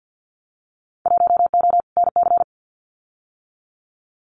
Morse code, heard as a keyed tone, about 50 wpm, 700 Hz. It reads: 1O NC
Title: Winter Field Day Exchanges 50wpm